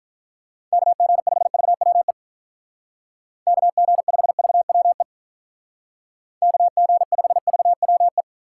Morse code, heard as a keyed tone, about 47 wpm, 700 Hz. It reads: KG5VWE KG5VWE KG5VWE